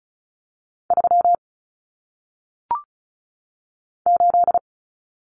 Morse code, heard as a keyed tone, about 35 wpm, 700 Hz. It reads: 3 8
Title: Common Cut Numbers - 1-Digit 35wpm